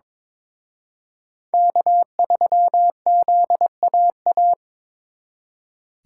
Morse code, heard as a keyed tone, about 26 wpm, 700 Hz. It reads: K3ZAA
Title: Speed Racing - USA 1x3 Call Signs 22wpm